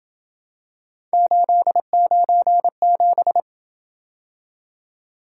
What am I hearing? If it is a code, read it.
Morse code, heard as a keyed tone, about 27 wpm, 700 Hz. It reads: 897